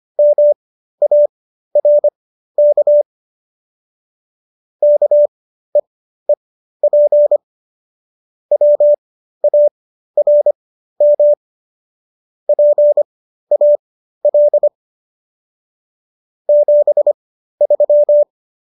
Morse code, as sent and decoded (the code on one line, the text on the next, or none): -- .- .-. -.- / -.- . .--. / .-- .- .-. -- / .--. .- .-.. / --... ...--
MARK KEEP WARM PAL 73